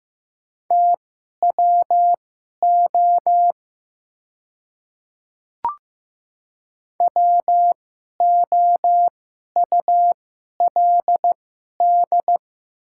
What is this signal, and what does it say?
Morse code, heard as a keyed tone, about 15 wpm, spaced as for 11 wpm, 700 Hz. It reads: TWO WOULD
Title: 2x Character Spacing - Top 100 Words - 15wpm